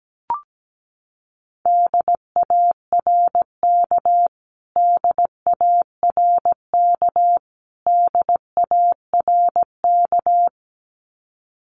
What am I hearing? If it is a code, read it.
Morse code, heard as a keyed tone, about 17 wpm, 700 Hz. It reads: DARK DARK DARK
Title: Top 500 Words Repeated 17wpm